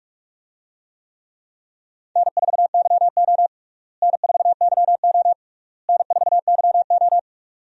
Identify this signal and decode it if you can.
Morse code, heard as a keyed tone, about 46 wpm, 700 Hz. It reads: N4YO N4YO N4YO
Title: USA Call Signs - 4 Characters - Repeated 3x - 45wpm